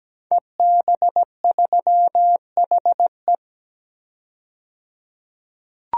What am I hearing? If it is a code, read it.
Morse code, heard as a keyed tone, about 17 wpm, 700 Hz. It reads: EB3HE
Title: DXCC-100 Call Signs - 5 Characters - 17wpm